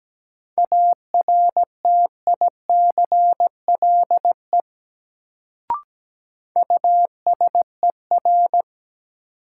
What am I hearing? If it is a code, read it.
Morse code, heard as a keyed tone, about 17 wpm, 700 Hz. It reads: ARTICLE USER
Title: Sentences from Top 1000 Words - Review Part 5 - New Word Drill - 17wpm